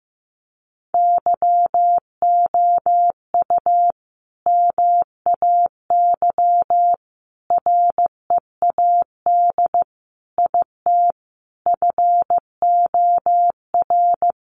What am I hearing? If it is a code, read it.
Morse code, heard as a keyed tone, about 15 wpm, 700 Hz. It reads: YOU MAY READ IT FOR
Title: Sentences from Top 600 Words 15wpm